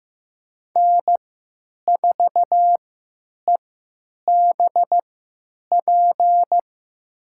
Morse code, heard as a keyed tone, about 15 wpm, spaced as for 9 wpm, 700 Hz. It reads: N4EBP
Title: USA Call Signs - 3x Character Spacing - 15wpm